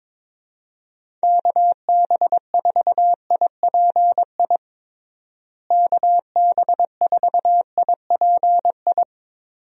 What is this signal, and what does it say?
Morse code, heard as a keyed tone, about 22 wpm, 700 Hz. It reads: KB4IPI KB4IPI
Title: USA Call Signs - 6 Characters - Repeated 2x - 22wpm